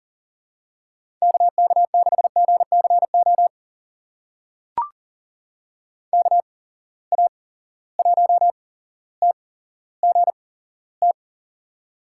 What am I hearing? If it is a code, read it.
Morse code, heard as a keyed tone, about 40 wpm, spaced as for 17 wpm, 700 Hz. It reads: KK6GCO KA1TGT
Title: USA Call Signs - 8x Character Spacing - 40wpm